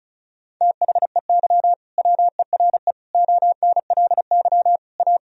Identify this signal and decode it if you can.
Morse code, heard as a keyed tone, about 34 wpm, 700 Hz. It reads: THEY WERE ONLY A
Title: Sentences from Top 1000 Words - Review Part 1 - 35wpm